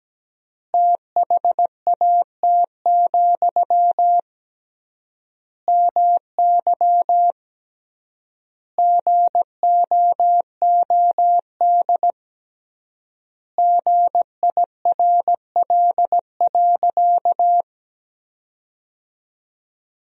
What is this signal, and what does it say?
Morse code, heard as a keyed tone, about 17 wpm, 700 Hz. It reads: THAT, MY GOOD GIRL.
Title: 3x Word-Spacing - Sentences from Top 400 Words - 17wpm